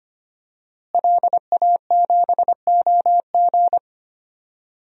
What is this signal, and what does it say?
Morse code, heard as a keyed tone, about 25 wpm, 700 Hz. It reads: LA7OG
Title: DXCC-100 Call Signs - 5 Characters - Repeated 3x - 25wpm